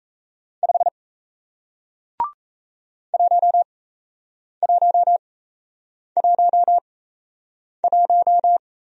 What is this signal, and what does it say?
Morse code, heard as a keyed tone, about 38 wpm, spaced as for 8 wpm, 700 Hz. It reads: 5 1111